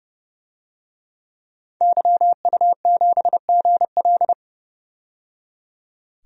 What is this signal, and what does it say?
Morse code, heard as a keyed tone, about 30 wpm, 700 Hz. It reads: YU7GL
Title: DXCC-100 Call Signs - 5 Characters - 30wpm